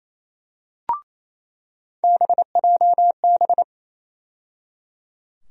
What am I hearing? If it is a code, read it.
Morse code, heard as a keyed tone, about 27 wpm, 700 Hz. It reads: BJB
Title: USA Call Sign Postfixes - 3 Letters 28wpm